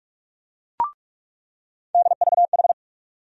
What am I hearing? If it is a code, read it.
Morse code, heard as a keyed tone, about 46 wpm, 700 Hz. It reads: DVH
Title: USA Call Sign Postfixes - 3 Letters 45wpm